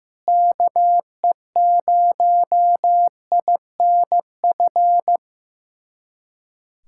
Morse code, heard as a keyed tone, about 15 wpm, 700 Hz. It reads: KE0INF
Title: USA Callsign DE Callsign 15wpm